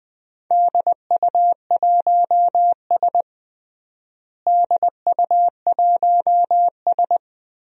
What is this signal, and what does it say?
Morse code, heard as a keyed tone, about 20 wpm, 700 Hz. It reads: DU1S DU1S
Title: DXCC-100 Call Signs - 4 Characters - Repeated 2x - 20wpm